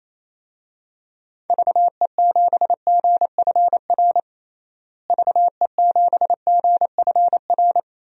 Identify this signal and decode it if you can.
Morse code, heard as a keyed tone, about 27 wpm, 700 Hz. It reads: VE7GFR VE7GFR
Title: DXCC-100 Call Signs - 6 Characters - Repeated 3x - 28wpm